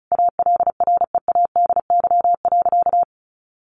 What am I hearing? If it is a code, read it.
Morse code, heard as a keyed tone, about 34 wpm, 700 Hz. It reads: ALREADY.
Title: Sentences from Top 700 Words 35wpm